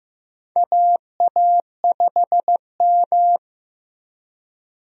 Morse code, heard as a keyed tone, about 15 wpm, 700 Hz. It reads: AA5M
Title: USA Call Signs - 4 Characters - Repeated 2x - 15wpm